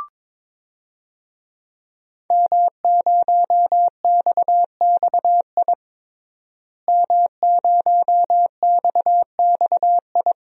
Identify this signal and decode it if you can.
Morse code, heard as a keyed tone, about 22 wpm, 700 Hz. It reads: M0XXI M0XXI
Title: DXCC-100 Call Signs - 5 Characters - Repeated 3x - 22wpm